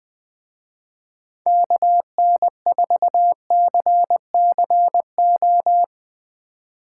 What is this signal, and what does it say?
Morse code, heard as a keyed tone, about 21 wpm, 700 Hz. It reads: KN4CCO
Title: Speed Racing - USA Call Signs 20wpm